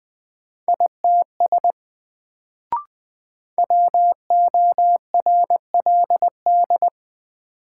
Morse code, heard as a keyed tone, about 20 wpm, 700 Hz. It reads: ITS WORLD
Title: Sets of 2 Words - New Word Drill - 20wpm